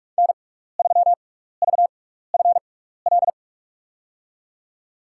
Morse code, heard as a keyed tone, about 44 wpm, spaced as for 15 wpm, 700 Hz. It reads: N3VFL
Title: USA Call Signs - 5 Characters - 6x Character Spacing - 45wpm